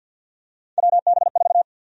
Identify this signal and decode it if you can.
Morse code, heard as a keyed tone, about 53 wpm, 700 Hz. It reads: WB4
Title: USA Call Sign Prefixes - Repeated 2x - 50wpm